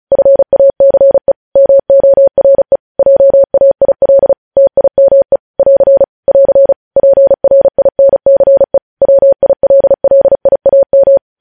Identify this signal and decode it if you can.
Morse code, heard as a keyed tone, about 36 wpm, 550 Hz. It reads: FACE MORE JAIL TIME <AR> <AR> PRINCE WILLIAM
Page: news headlines in morse code at 35 wpm